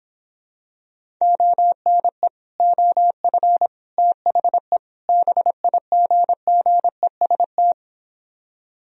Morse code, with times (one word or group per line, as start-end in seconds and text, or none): 1.2-2.3 ONE
2.6-3.7 OF
4.0-4.8 THE
5.1-7.7 BIGGEST